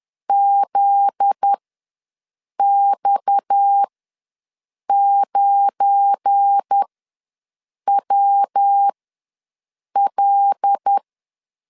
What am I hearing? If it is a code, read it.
Morse code, heard as a keyed tone, about 10 wpm, spaced as for 6 wpm, 800 Hz. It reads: ZX9WL
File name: LEZIONI / 8 YJ45 / 8 YJ4 ws 10wpm 4020